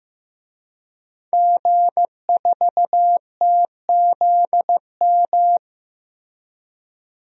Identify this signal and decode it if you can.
Morse code, heard as a keyed tone, about 15 wpm, 700 Hz. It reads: G4TZM
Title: DXCC-100 Call Signs - 5 Characters - Repeated 2x - 15wpm